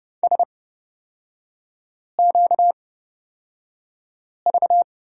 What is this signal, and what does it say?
Morse code, heard as a keyed tone, about 30 wpm, spaced as for 5 wpm, 700 Hz. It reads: SQV
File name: course / 30:10wpm / 236 Lesson 38e - Letter q - Three Characters 30wpm